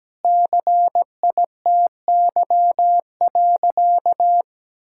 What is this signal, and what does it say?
Morse code, heard as a keyed tone, about 17 wpm, 700 Hz. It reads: CITY.
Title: Sentences from Top 1000 Words - Review Part 5 - 17wpm